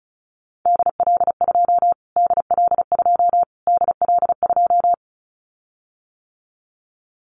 Morse code, heard as a keyed tone, about 34 wpm, 700 Hz. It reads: DL2 DL2 DL2
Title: Callsign Prefixes for Top 25 DXCC - Repeated 3x - 35wpm